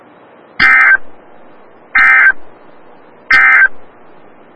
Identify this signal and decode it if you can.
Emergency Alert System (SAME) tones, heard as harsh buzzing bursts. It is heard from 0.5 to 3.8 s.
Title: Des Moines, IA NOAA Weather Radio - WXL57